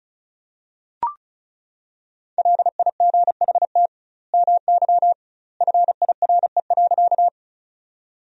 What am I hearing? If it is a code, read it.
Morse code, heard as a keyed tone, about 34 wpm, 700 Hz. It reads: LIGHT MY FIRE.
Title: 2x Word-Spacing - Sentences from Top 400 Words - 35wpm